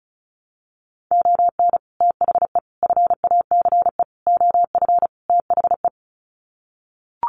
Morse code, heard as a keyed tone, about 34 wpm, 700 Hz. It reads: ON THE FACE OF THE
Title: Sets of 5 Words Encore2 35wpm